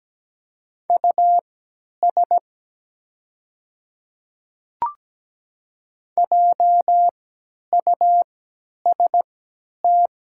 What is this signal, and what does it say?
Morse code, heard as a keyed tone, about 17 wpm, spaced as for 10 wpm, 700 Hz. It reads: US JUST
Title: 3x Character Spacing - Top 200 Words - 17wpm